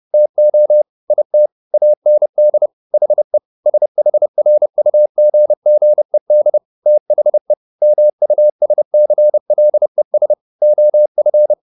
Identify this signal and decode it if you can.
Morse code, heard as a keyed tone, about 30 wpm, 600 Hz. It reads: TO IT AND HE SHRUGGED THE MUSCLES OF
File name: the old man and the sea / the old man and the sea 1 30wpm